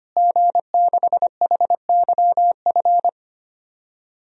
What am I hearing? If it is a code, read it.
Morse code, heard as a keyed tone, about 25 wpm, 700 Hz. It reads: G6HYF
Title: DXCC-100 Call Signs - 5 Characters - Repeated 3x - 25wpm